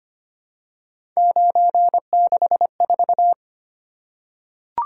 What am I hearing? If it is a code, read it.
Morse code, heard as a keyed tone, about 25 wpm, 700 Hz. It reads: 964
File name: Three Letter-Number Combinations - 25wpm